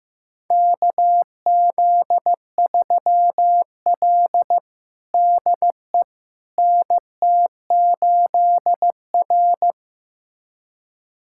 Morse code, heard as a keyed tone, about 15 wpm, 700 Hz. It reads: KZ3L DE NT8R
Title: USA Callsign DE Callsign - 4 Characters - 15wpm